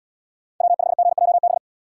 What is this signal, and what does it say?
Morse code, heard as a keyed tone, about 73 wpm, 700 Hz. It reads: W5GJB